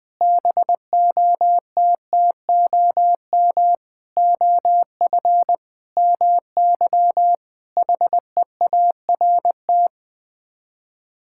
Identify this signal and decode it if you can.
Morse code, heard as a keyed tone, about 21 wpm, 700 Hz. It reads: BOTTOM OF MY HEART
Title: Sets of 8 Words - 20wpm